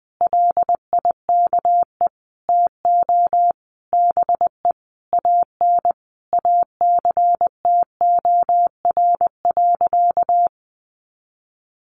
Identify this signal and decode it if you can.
Morse code, heard as a keyed tone, about 20 wpm, 700 Hz. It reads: LIKE TO BE AN ACTOR.
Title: Sentences from Top 1500 Words 20wpm